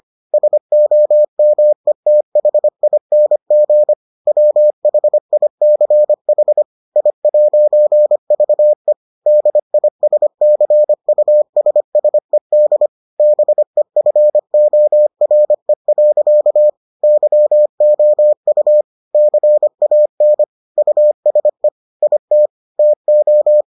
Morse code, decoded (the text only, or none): SOMETHING WHICH I'VE DISCUSSED BEFORE. YOU CAN USE IT TO